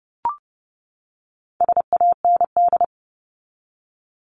Morse code, heard as a keyed tone, about 30 wpm, 700 Hz. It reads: SAND
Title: Top 1501-1750 Words 30wpm